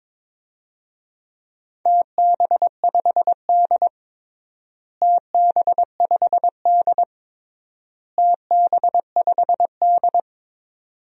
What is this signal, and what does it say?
Morse code, heard as a keyed tone, about 22 wpm, 700 Hz. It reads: TB5D TB5D TB5D
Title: DXCC-100 Call Signs - 4 Characters - Repeated 3x - 22wpm